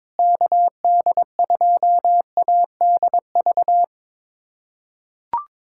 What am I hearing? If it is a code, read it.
Morse code, heard as a keyed tone, about 21 wpm, 700 Hz. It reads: KD2ADV
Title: USA Callsign DE Callsign - 6 Characters - 22wpm